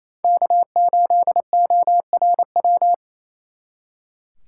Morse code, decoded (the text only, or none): K8ORW